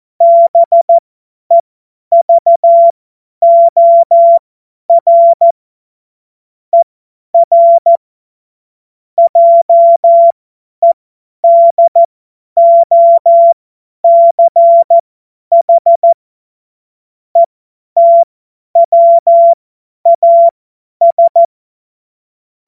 Morse code, heard as a keyed tone, about 14 wpm, spaced as for 10 wpm, 680 Hz. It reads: BEVOR ER JEDOCH ETWAS
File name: Morsekurs / Witze / Witz 15. August 2023 50 F70